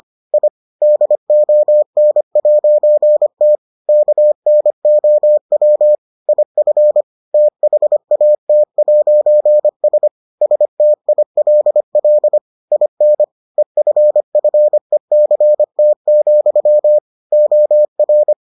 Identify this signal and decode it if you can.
Morse code, heard as a keyed tone, about 25 wpm, 600 Hz. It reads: I DON'T KNOW IF THAT'S STILL IN EFFECT, OR